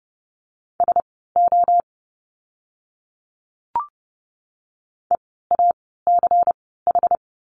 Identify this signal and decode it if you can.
Morse code, heard as a keyed tone, about 30 wpm, spaced as for 17 wpm, 700 Hz. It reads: SO EACH
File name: course / 30:10wpm / 230 Lesson 37f - Letter x - Words 30wpm